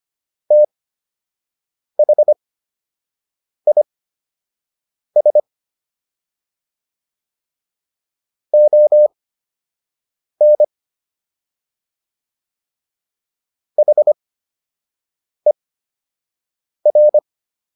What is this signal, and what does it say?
Morse code, heard as a keyed tone, about 25 wpm, spaced as for 6 wpm, 600 Hz. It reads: THIS ON HER